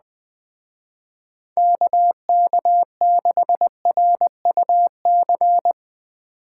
Morse code, heard as a keyed tone, about 20 wpm, 700 Hz. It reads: KK6RUC
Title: Speed Racing - USA Call Signs 17wpm